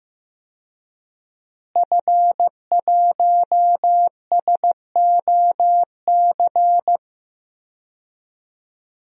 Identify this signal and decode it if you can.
Morse code, heard as a keyed tone, about 16 wpm, 700 Hz. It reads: F1SOC
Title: DXCC-100 Call Signs - 5 Characters - Repeated 2x - 15wpm